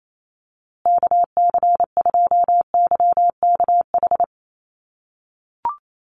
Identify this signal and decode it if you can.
Morse code, heard as a keyed tone, about 28 wpm, 700 Hz. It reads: KC2YKH
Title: USA Call Signs Repeated 2x 28wpm